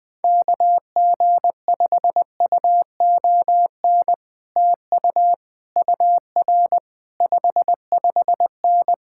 Morse code, heard as a keyed tone, about 21 wpm, 700 Hz. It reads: KG5UON TU UR 55N
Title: Speed-Racing Common QSO Phrases - 20wpm